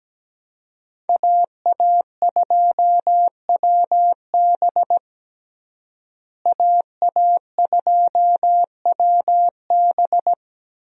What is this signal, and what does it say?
Morse code, heard as a keyed tone, about 17 wpm, 700 Hz. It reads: AA2WB AA2WB
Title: USA Call Signs - 5 Characters - Repeated 3x - 17wpm